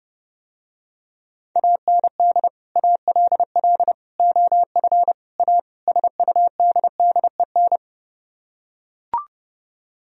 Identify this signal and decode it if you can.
Morse code, heard as a keyed tone, about 30 wpm, 700 Hz. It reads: AND ALL OF A SUDDEN